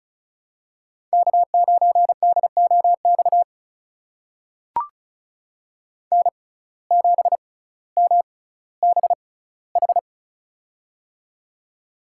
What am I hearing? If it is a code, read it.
Morse code, heard as a keyed tone, about 36 wpm, spaced as for 18 wpm, 700 Hz. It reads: K9DOX N7MBH
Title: USA Call Signs - 5 Characters - 6x Character Spacing - 35wpm